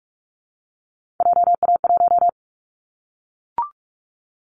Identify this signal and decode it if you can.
Morse code, heard as a keyed tone, about 44 wpm, 700 Hz. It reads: JA1
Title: Callsign Prefixes for Top 25 DXCC - Repeated 3x - 45wpm